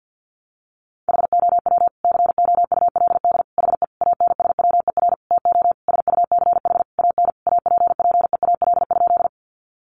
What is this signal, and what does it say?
Morse code, heard as a keyed tone, about 50 wpm, 700 Hz. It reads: HOW COULD HE ANSWER TO SUCH AN APPEAL?
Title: Sentences from Top 1300 Words 50wpm